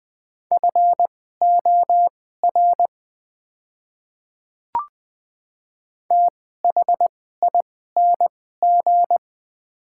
Morse code, heard as a keyed tone, about 21 wpm, spaced as for 15 wpm, 700 Hz. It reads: FOR THING
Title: 2x Character Spacing - Top 100 Words - 20wpm